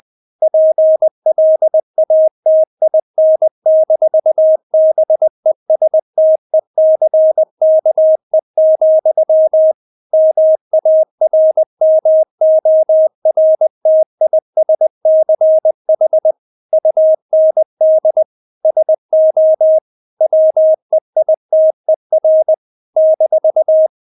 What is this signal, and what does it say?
Morse code, heard as a keyed tone, about 20 wpm, 630 Hz. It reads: PLATIN-BESTECKE, MARMORTISCH UND SO WEITER -
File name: Witz1 20wpm